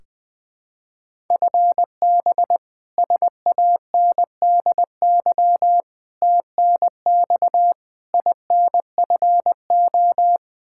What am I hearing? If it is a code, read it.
Morse code, heard as a keyed tone, about 20 wpm, 700 Hz. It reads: FB SANDY TNX INFO
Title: Practice QSO - Exchange 3 - More Info - 20wpm